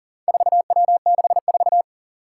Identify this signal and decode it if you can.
Morse code, heard as a keyed tone, about 37 wpm, 700 Hz. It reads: 4W64
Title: Four Challenging Letter-Number Combinations 40wpm